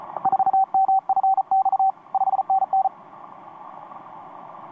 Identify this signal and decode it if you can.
Morse code, heard as a keyed tone, about 34 wpm, 770 Hz. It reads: 4MFX 5NN
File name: t46rrc i5fln etc 15m cw 18feb2013 1248